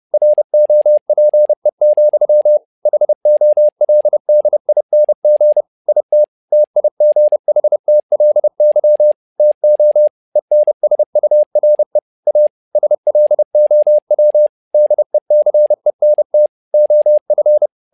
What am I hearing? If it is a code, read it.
Morse code, heard as a keyed tone, about 30 wpm, 600 Hz. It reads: ROPE, HOLDING IT TIGHTLY TO ENSURE A SLOW DECENT OF
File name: learning center / tower trouble / tower trouble 30wpm